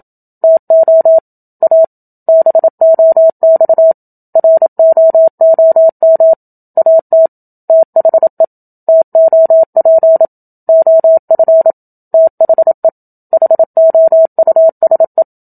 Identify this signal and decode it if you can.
Morse code, heard as a keyed tone, about 27 wpm, 650 Hz. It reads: TO A BOX ROOM AT THE TOP OF THE HOUSE
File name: Book Two / 01-Chapter 1 - 27 WPM